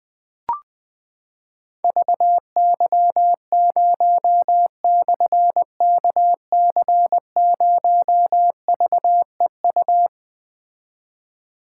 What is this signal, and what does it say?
Morse code, heard as a keyed tone, about 20 wpm, 700 Hz. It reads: VY0/KC0VEU
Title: Difficult Call Signs 20wpm